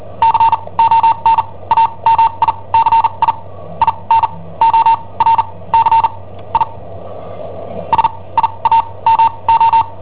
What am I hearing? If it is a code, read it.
Morse code, heard as a keyed tone, about 43 wpm, spaced as for 26 wpm, 910 Hz. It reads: CON AMICI IN OR CI SIAMO
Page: Cassetta esercizi preparata da I7OHP per IW5ELP